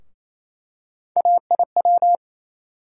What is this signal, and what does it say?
Morse code, heard as a keyed tone, about 27 wpm, 700 Hz. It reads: AIW
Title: USA Call Sign Postfixes - 3 Letters 28wpm